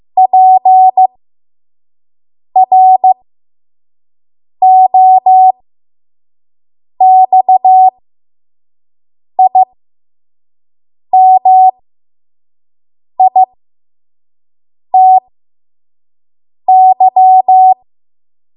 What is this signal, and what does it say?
Morse code, heard as a keyed tone, about 15 wpm, spaced as for 5 wpm, 750 Hz. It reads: PROXIMITY
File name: Morse / 05WPM